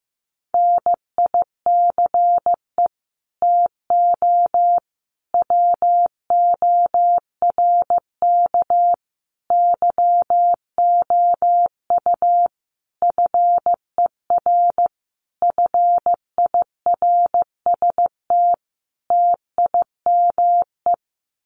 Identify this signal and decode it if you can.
Morse code, heard as a keyed tone, about 14 wpm, 700 Hz. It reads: NICE TO WORK YOU FER FIRST TIME
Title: Common QSO Phrases 15wpm